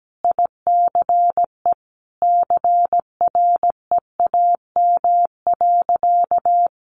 Morse code, heard as a keyed tone, about 17 wpm, 700 Hz. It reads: ICE CREAM.
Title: Sentences from Top 1500 Words 17wpm